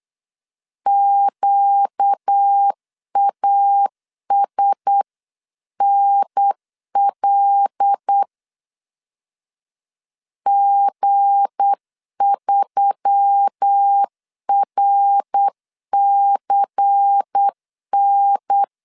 Morse code, heard as a keyed tone, about 8 wpm, 800 Hz. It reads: QASNL G3RCN
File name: LEZIONI / 7 GF9Z / 7 G 8wpm 3118